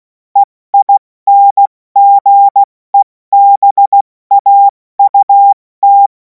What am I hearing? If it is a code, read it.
Morse code, heard as a keyed tone, about 16 wpm, 800 Hz. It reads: EINGEBAUT
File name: Morsekurs / Witze / Witz 28. April 2026 70 F80